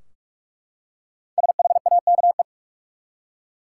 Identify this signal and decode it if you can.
Morse code, heard as a keyed tone, about 44 wpm, 700 Hz. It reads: SHAKE
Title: Five Letter Words - 45wpm